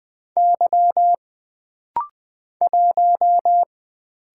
Morse code, heard as a keyed tone, about 20 wpm, 700 Hz. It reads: Y 1
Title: Single Letter-Number - Rapid-Fire 20wpm